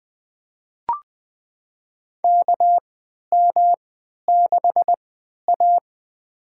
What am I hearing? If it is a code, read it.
Morse code, heard as a keyed tone, about 20 wpm, spaced as for 11 wpm, 700 Hz. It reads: KM6A
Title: USA Call Signs - 4 Characters - 3x Character Spacing - 20wpm